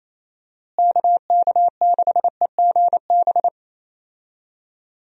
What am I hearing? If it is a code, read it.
Morse code, heard as a keyed tone, about 28 wpm, 700 Hz. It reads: KK6EGB
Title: USA Call Signs - 6 Characters - Repeated 3x - 28wpm